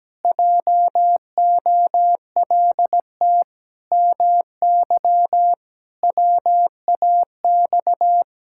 Morse code, heard as a keyed tone, about 17 wpm, 700 Hz. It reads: JOLT MY WAX
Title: Pangram Challenge v2 - 17wpm